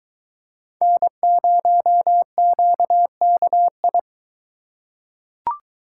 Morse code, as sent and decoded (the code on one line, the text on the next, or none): -. ----- --.- -.- ..
N0QKI